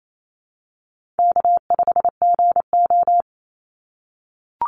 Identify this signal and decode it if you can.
Morse code, heard as a keyed tone, about 28 wpm, 700 Hz. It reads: K5GO
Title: Easy Call Signs - 28wpm